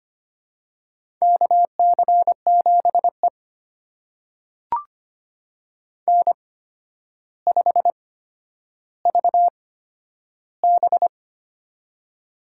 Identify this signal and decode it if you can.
Morse code, heard as a keyed tone, about 25 wpm, spaced as for 11 wpm, 700 Hz. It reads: KC7E N5VB
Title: USA Call Signs - 4 Characters - 8x Character Spacing - 25wpm